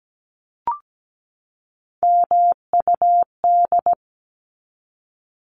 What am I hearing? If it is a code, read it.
Morse code, heard as a keyed tone, about 17 wpm, 700 Hz. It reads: MUD
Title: Top 1751-2000 Words 17wpm